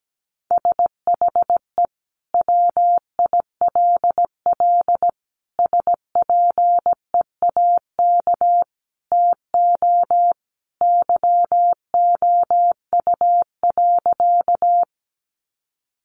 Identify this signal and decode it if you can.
Morse code, heard as a keyed tone, about 17 wpm, 700 Hz. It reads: SHE WILL SPEAK TO YOU.
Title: Sentences from Top 1100 Words 17wpm